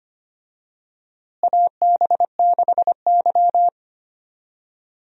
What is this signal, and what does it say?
Morse code, heard as a keyed tone, about 25 wpm, 700 Hz. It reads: AB6Y
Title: USA Call Signs - 4 Characters - Repeated 2x - 25wpm